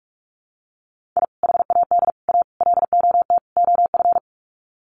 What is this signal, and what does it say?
Morse code, heard as a keyed tone, about 44 wpm, 700 Hz. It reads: I HAD A LOT OF
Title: Sets of 5 Words Encore 45wpm